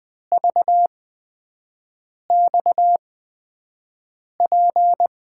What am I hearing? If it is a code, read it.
Morse code, heard as a keyed tone, about 20 wpm, spaced as for 5 wpm, 700 Hz. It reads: VXP
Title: USA Call Signs - 8x Character Spacing - 20wpm